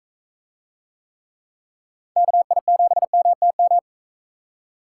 Morse code, heard as a keyed tone, about 43 wpm, 700 Hz. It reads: KI7MTM